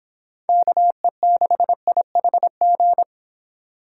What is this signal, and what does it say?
Morse code, heard as a keyed tone, about 26 wpm, 700 Hz. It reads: KE6IHG